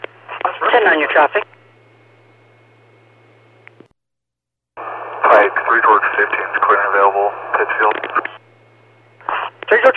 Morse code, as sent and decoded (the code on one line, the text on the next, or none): none